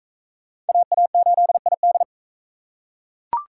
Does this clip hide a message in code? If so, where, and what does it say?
Yes: Morse code from 0.7 to 2.0 s, heard as a keyed tone, about 37 wpm, 700 Hz. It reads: AA8ID